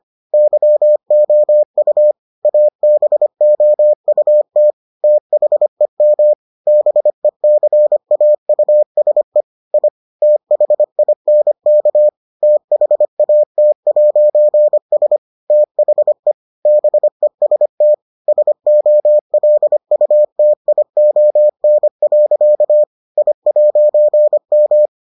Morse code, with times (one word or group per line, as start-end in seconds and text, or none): 0.3-2.1 YOU
2.5-4.7 ABOUT
5.0-6.3 THEM
6.7-9.4 BECAUSE
9.7-9.9 I
10.2-12.1 THINK
12.4-15.2 THAT'S
15.5-16.3 THE
16.7-17.9 BEST
18.3-22.8 SOLUTION.
23.2-24.9 I'M